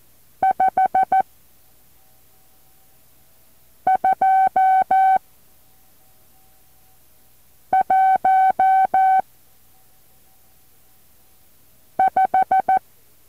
Morse code, heard as a keyed tone, about 14 wpm, 760 Hz. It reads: 5 2 1 5
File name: intl morse code2